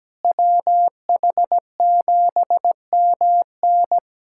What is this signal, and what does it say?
Morse code, heard as a keyed tone, about 17 wpm, 700 Hz. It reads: WH7MN
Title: USA Callsign DE Callsign - 5 Characters - 17wpm